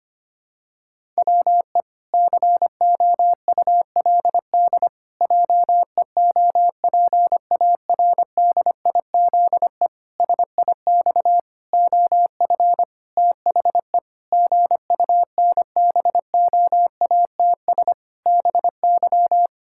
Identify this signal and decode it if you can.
Morse code, heard as a keyed tone, about 25 wpm, 700 Hz. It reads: WE COULD JEOPARDIZE SIX OF THE GUNBOATS BY
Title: Pangram Challenge v2 - 25wpm